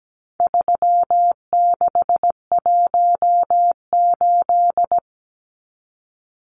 Morse code, heard as a keyed tone, about 17 wpm, 700 Hz. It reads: 3618
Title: Numbers - 4-Digit 17wpm